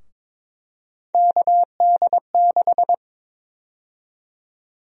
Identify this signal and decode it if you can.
Morse code, heard as a keyed tone, about 21 wpm, 700 Hz. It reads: KD6
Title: USA Call Sign Prefixes - Repeated 2x - 22wpm